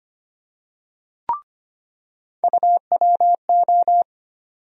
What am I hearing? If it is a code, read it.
Morse code, heard as a keyed tone, about 25 wpm, 700 Hz. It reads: UWO
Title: USA Call Sign Postfixes - 3 Letters 25wpm